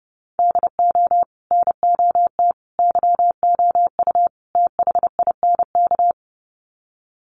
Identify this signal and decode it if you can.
Morse code, heard as a keyed tone, about 30 wpm, 700 Hz. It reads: DO NOT YOU THINK
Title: Sets of 4 Words 30wpm